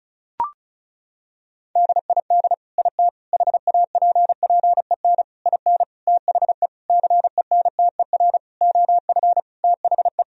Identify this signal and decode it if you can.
Morse code, heard as a keyed tone, about 34 wpm, 700 Hz. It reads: DID IT HAPPEN IN THE CENTER OF THE
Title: Sentences from Top 1000 Words - Review Part 2 - 35wpm